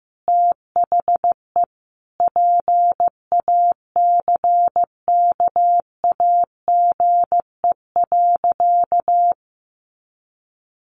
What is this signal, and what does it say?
Morse code, heard as a keyed tone, about 15 wpm, 700 Hz. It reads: THE PACKAGE.
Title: Additional Easy Phrases 15wpm